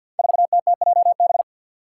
Morse code, heard as a keyed tone, about 47 wpm, 700 Hz. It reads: 4TTJB
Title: Five Challenging Letter-Number Combinations 50wpm